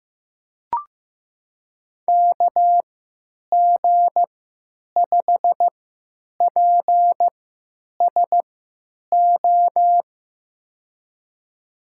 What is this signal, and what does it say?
Morse code, heard as a keyed tone, about 15 wpm, spaced as for 9 wpm, 700 Hz. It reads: KG5PSO
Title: USA Call Signs - 3x Character Spacing - 15wpm